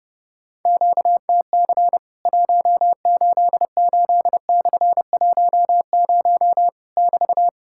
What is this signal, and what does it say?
Morse code, heard as a keyed tone, about 30 wpm, 700 Hz. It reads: QTC 188/10 <BT>
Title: Worked All Europe DX Contest - QTC Exchange - 30wpm